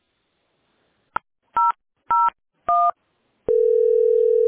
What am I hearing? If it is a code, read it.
DTMF, heard as telephone touch tones, tones of about 210 ms. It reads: 001